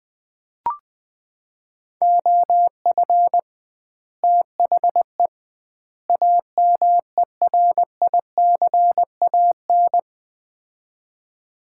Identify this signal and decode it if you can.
Morse code, heard as a keyed tone, about 21 wpm, 700 Hz. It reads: OF THE AMERICAN
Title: 2x Word-Spacing - Sets of 3 Words - 20wpm